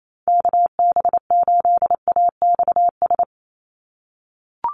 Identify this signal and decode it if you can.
Morse code, heard as a keyed tone, about 28 wpm, 700 Hz. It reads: KB8AXS
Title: USA Call Signs 28wpm